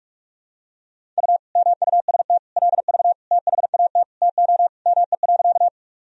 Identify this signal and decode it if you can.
Morse code, heard as a keyed tone, about 41 wpm, 700 Hz. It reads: U MUST LV THAT TO ME.
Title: Sentences from Top 400 Words and Abbreviations 45wpm